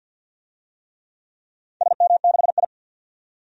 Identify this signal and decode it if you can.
Morse code, heard as a keyed tone, about 52 wpm, 700 Hz. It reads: SM6I
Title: DXCC-100 Call Signs - 4 Characters - 50wpm